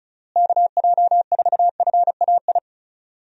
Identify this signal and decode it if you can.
Morse code, heard as a keyed tone, about 34 wpm, 700 Hz. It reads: KJ4FAI
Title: USA Call Signs - 6 Characters - Repeated 3x - 35wpm